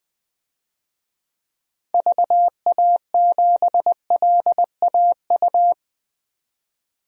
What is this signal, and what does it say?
Morse code, heard as a keyed tone, about 20 wpm, 700 Hz. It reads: VA7LAU
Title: DXCC-100 Call Signs - 6 Characters - 20wpm